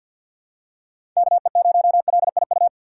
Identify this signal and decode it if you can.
Morse code, heard as a keyed tone, about 52 wpm, 700 Hz. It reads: KE0LIU
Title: USA Call Signs - 3x Character Spacing - 50wpm